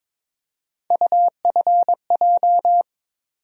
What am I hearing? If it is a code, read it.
Morse code, heard as a keyed tone, about 22 wpm, 700 Hz. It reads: UFJ
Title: USA Call Sign Postfixes - 3 Letters 22wpm